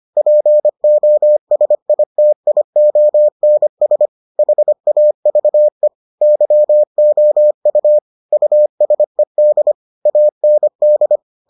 Morse code, heard as a keyed tone, about 25 wpm, 600 Hz. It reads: POSITIONS HAVE YOU USED AND